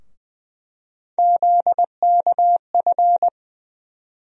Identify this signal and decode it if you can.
Morse code, heard as a keyed tone, about 21 wpm, 700 Hz. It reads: ZKF
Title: USA Call Sign Postfixes - 3 Letters 20wpm